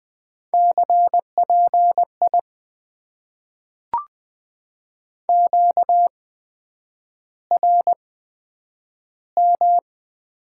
Morse code, heard as a keyed tone, about 20 wpm, spaced as for 9 wpm, 700 Hz. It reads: CPI QRM
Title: Top 90 QSO Elements - 8x Character Spacing - 20wpm